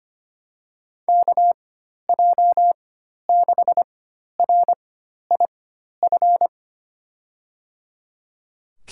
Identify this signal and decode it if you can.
Morse code, heard as a keyed tone, about 25 wpm, spaced as for 12 wpm, 700 Hz. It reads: KJ6RIF
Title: USA Call Signs - 6 Characters - 4x Character Spacing - 25wpm